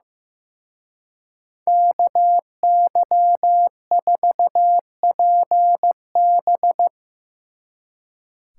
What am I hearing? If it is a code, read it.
Morse code, heard as a keyed tone, about 16 wpm, 700 Hz. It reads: KY4PB
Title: Speed Racing - USA 2x2 Call Signs 15wpm